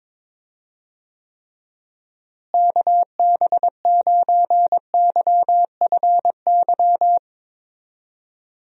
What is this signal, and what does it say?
Morse code, heard as a keyed tone, about 22 wpm, 700 Hz. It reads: KB9YFY
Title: USA Call Signs - 6 Characters - Repeated 3x - 22wpm